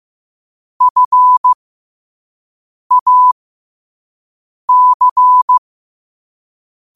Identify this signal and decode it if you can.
Morse code, heard as a keyed tone, about 15 wpm, spaced as for 5 wpm, 1000 Hz. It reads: FAC
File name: qotdfiles / 5wpm